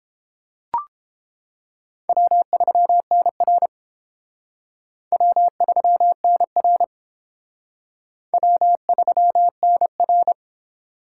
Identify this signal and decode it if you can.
Morse code, heard as a keyed tone, about 29 wpm, 700 Hz. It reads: W3NR W3NR W3NR